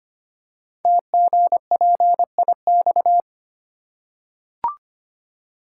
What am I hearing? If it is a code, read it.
Morse code, heard as a keyed tone, about 26 wpm, 700 Hz. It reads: TGPIX
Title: Code Groups - 5 Letters - 25wpm